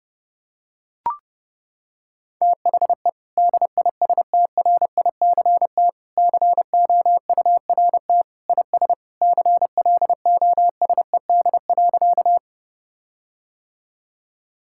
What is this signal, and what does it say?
Morse code, heard as a keyed tone, about 30 wpm, 700 Hz. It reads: THE DISTRICT COURT IS CLOSED.
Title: Sentences from Top 1600 Words 30wpm